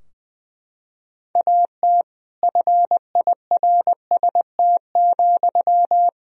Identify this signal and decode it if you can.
Morse code, heard as a keyed tone, about 20 wpm, 700 Hz. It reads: AT FIRST,
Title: Sentences from Top 1000 Words - Review Part 4 - 20wpm